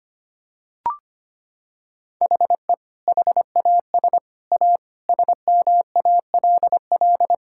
Morse code, heard as a keyed tone, about 25 wpm, 700 Hz. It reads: HE HAS A SMALL
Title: Sentences from Top 1000 Words - Review Part 6 - 25wpm